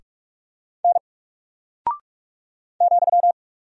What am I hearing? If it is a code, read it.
Morse code, heard as a keyed tone, about 44 wpm, 700 Hz. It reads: N ,
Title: Single Character (Letter:Number:Punctuation:Prosign) Alt 45wpm